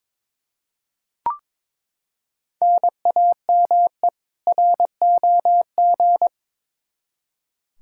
Morse code, heard as a keyed tone, about 22 wpm, 700 Hz. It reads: NAME ROG
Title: Practice QSO - Exchange 1 - Beginning - 22wpm